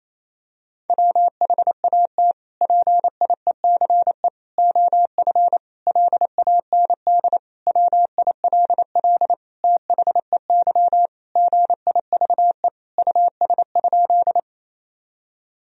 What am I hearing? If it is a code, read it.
Morse code, heard as a keyed tone, about 28 wpm, 700 Hz. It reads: WHAT PIECE OF LAND WILL THEY GIVE US?
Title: Sentences from Top 1000 Words - Review Part 2 - 28wpm